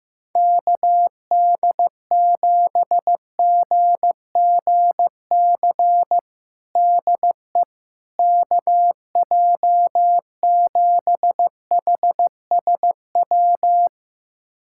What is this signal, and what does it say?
Morse code, heard as a keyed tone, about 15 wpm, 700 Hz. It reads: KD7GGC DE KJ7HSW
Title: USA Callsign DE Callsign - 6 Characters - 15wpm